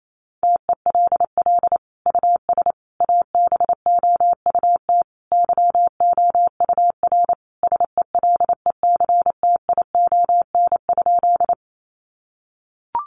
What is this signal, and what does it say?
Morse code, heard as a keyed tone, about 28 wpm, 700 Hz. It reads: TELL US ABOUT YOUR SELECTION?
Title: Sentences from Top 1400 Words 28wpm